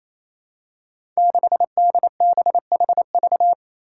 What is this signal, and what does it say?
Morse code, heard as a keyed tone, about 28 wpm, 700 Hz. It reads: 6DBHV
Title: Five Challenging Letter-Number Combinations 28wpm